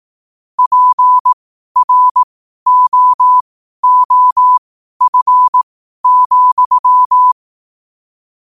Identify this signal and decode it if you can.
Morse code, heard as a keyed tone, about 18 wpm, spaced as for 12 wpm, 1000 Hz. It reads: PROOF,
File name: qotdfiles / 12wpm